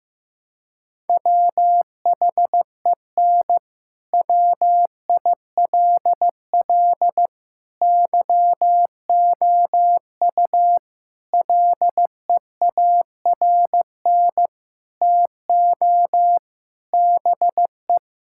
Morse code, heard as a keyed tone, about 16 wpm, 700 Hz. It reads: WHEN WILL YOU LEARN TO BE
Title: Sentences from Top 1000 Words - Review Part 6 - 15wpm